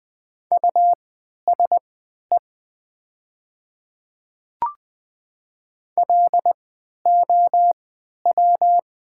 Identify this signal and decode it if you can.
Morse code, heard as a keyed tone, about 20 wpm, spaced as for 11 wpm, 700 Hz. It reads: USE LOW
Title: 3x Character Spacing - Top 300 Words - 20wpm